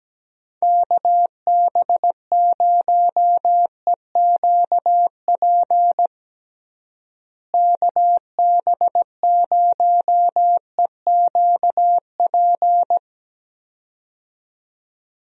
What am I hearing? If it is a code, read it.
Morse code, heard as a keyed tone, about 17 wpm, 700 Hz. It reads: KB0EQP KB0EQP
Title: USA Call Signs - 6 Characters - Repeated 2x - 17wpm